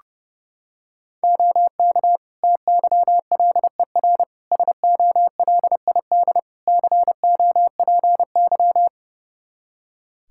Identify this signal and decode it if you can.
Morse code, heard as a keyed tone, about 30 wpm, 700 Hz. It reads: OK TYLER SOLID COPY
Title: Practice QSO - Exchange 3 - More Info - 30wpm